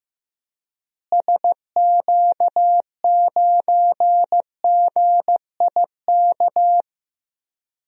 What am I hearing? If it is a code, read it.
Morse code, heard as a keyed tone, about 16 wpm, 700 Hz. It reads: SQ9GIK
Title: DXCC-100 Call Signs - 6 Characters - Repeated 3x - 15wpm